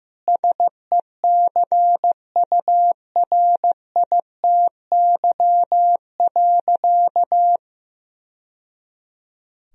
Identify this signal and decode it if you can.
Morse code, heard as a keyed tone, about 15 wpm, 700 Hz. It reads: SECURITY.
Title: Sentences from Top 1000 Words - Review Part 4 - 15wpm